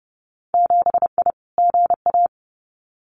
Morse code, heard as a keyed tone, about 32 wpm, 700 Hz. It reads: 7I GA
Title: Winter Field Day Exchanges Encore 30wpm